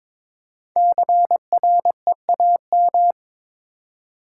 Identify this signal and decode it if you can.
Morse code, heard as a keyed tone, about 22 wpm, 700 Hz. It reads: CREAM
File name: 5 Letter Words - First 3 Letters Given - 22wpm